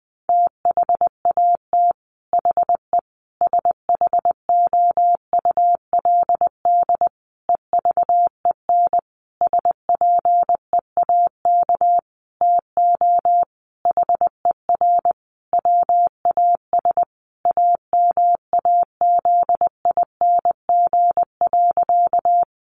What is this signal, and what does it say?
Morse code, heard as a keyed tone, about 20 wpm, 700 Hz. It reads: THAT HE SHOULD EVEN SPEAK TO HER WAS AMAZING.
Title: Sentences from Top 1500 Words 20wpm